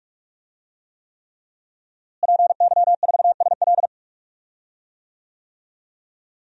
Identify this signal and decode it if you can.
Morse code, heard as a keyed tone, about 42 wpm, 700 Hz. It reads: PY4SL
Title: DXCC-100 Call Signs - 5 Characters - Repeated 3x - 45wpm